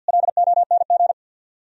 Morse code, heard as a keyed tone, about 67 wpm, spaced as for 58 wpm, 700 Hz. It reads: LONG
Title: Speed-Racing - Sets of 2 Words 50wpm